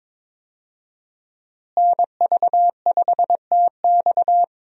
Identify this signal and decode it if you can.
Morse code, heard as a keyed tone, about 22 wpm, 700 Hz. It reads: NV5TX
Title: USA Call Signs - 5 Characters - Repeated 2x - 22wpm